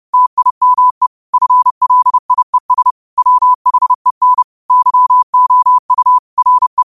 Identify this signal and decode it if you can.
Morse code, heard as a keyed tone, about 30 wpm, 1000 Hz. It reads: TIME FLIES WHEN YOU RE
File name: qotdfiles / 30wpm